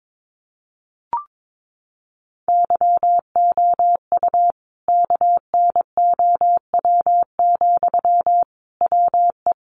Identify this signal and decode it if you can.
Morse code, heard as a keyed tone, about 22 wpm, 700 Hz. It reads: YOU KNOW, WE
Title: Sentences from Top 600 Words 22wpm